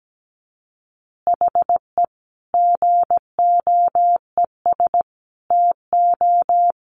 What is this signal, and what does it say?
Morse code, heard as a keyed tone, about 17 wpm, 700 Hz. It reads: HE GOES TO
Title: Additional Easy Phrases 17wpm